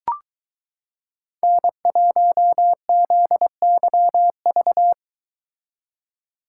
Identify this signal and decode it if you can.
Morse code, heard as a keyed tone, about 22 wpm, 700 Hz. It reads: N1ZYV